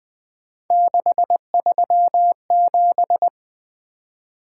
Morse code, heard as a keyed tone, about 20 wpm, 700 Hz. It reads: 637
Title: Three Letter-Number Combinations - 20wpm